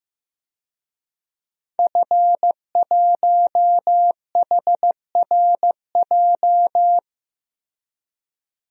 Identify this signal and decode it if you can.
Morse code, heard as a keyed tone, about 15 wpm, 700 Hz. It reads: F1HRJ
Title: DXCC-100 Call Signs - 5 Characters - 15wpm